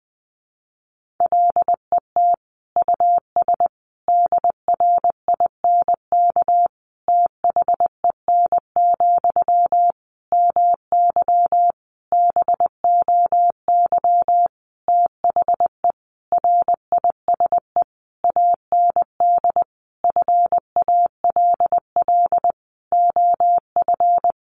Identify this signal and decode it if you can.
Morse code, heard as a keyed tone, about 20 wpm, 700 Hz. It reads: LET US DRINK THEN, MY BOY THE RISE AND FALL OF
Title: Sentences from Top 1500 Words 20wpm